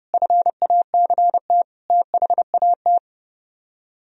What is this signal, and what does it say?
Morse code, heard as a keyed tone, about 36 wpm, 700 Hz. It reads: FACT THAT
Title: Speed-Racing - Sets of 4 Words 30wpm